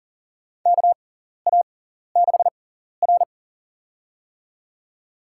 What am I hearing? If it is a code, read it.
Morse code, heard as a keyed tone, about 37 wpm, spaced as for 14 wpm, 700 Hz. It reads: KA6R
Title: USA Call Signs - 6x Character Spacing - 40wpm